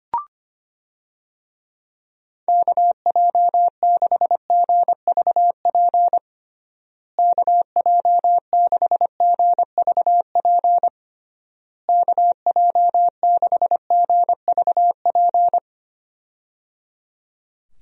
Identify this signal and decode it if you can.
Morse code, heard as a keyed tone, about 25 wpm, 700 Hz. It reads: KJ6GVP KJ6GVP KJ6GVP
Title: USA Call Signs - 6 Characters - Repeated 3x - 25wpm